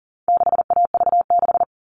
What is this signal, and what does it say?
Morse code, heard as a keyed tone, about 40 wpm, 700 Hz. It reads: 6AV6
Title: All Cut Numbers - 4-Digit 40wpm